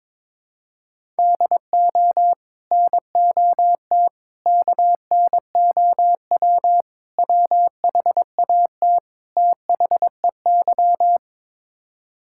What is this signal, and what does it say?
Morse code, heard as a keyed tone, about 22 wpm, 700 Hz. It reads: DO NOT KNOW WHAT THEY
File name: practice / 22wpm / Build A Phrase - 5-Word Phrases - 22wpm